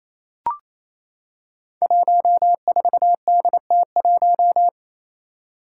Morse code, heard as a keyed tone, about 28 wpm, 700 Hz. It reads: 14DT1
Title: Five Challenging Letter-Number Combinations 28wpm